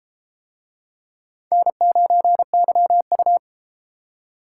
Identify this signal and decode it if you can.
Morse code, heard as a keyed tone, about 32 wpm, 700 Hz. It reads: N9YU